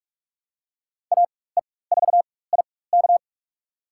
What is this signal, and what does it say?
Morse code, heard as a keyed tone, about 44 wpm, spaced as for 21 wpm, 700 Hz. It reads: AE4IK
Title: USA Call Signs - 5 Characters - 4x Character Spacing - 45wpm